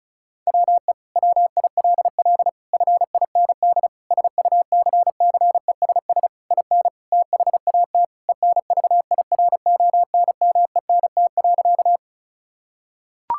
Morse code, heard as a keyed tone, about 34 wpm, 700 Hz. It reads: WE WILL FIND SUCCESS IN THAT ENVIRONMENT.
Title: Sentences from Top 1000 Words - Review Part 5 - 35wpm